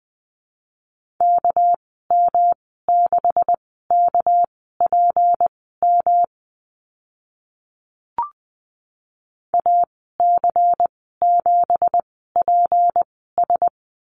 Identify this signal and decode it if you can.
Morse code, heard as a keyed tone, about 19 wpm, spaced as for 14 wpm, 700 Hz. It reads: KM6KPM AC7PS
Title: 200 Lesson 32g - Letter k - Call Signs 20wpm